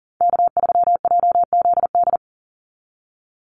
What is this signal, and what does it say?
Morse code, heard as a keyed tone, about 40 wpm, 700 Hz. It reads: K3JZD
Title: NA SOTA Chaser Callsigns 40wpm